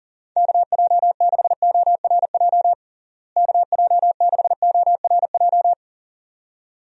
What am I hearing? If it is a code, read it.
Morse code, heard as a keyed tone, about 40 wpm, 700 Hz. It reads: KJ6ORJ KJ6ORJ
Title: USA Call Signs - 6 Characters - Repeated 2x - 40wpm